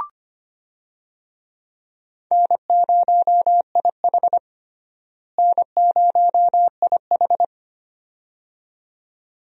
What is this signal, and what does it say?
Morse code, heard as a keyed tone, about 25 wpm, 700 Hz. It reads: N0IH N0IH
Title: USA Call Signs - 4 Characters - Repeated 2x - 25wpm